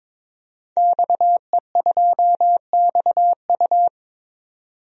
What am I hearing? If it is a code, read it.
Morse code, heard as a keyed tone, about 22 wpm, 700 Hz. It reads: XE2XU
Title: DXCC-100 Call Signs - 5 Characters - 22wpm